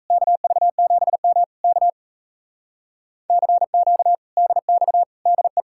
Morse code, heard as a keyed tone, about 37 wpm, 700 Hz. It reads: KV7M K CQ DX DE